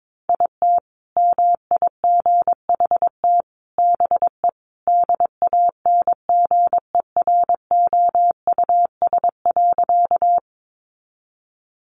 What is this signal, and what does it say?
Morse code, heard as a keyed tone, about 22 wpm, 700 Hz. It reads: IT MIGHT BE DANGEROUS.
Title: Sentences from Top 1200 Words 22wpm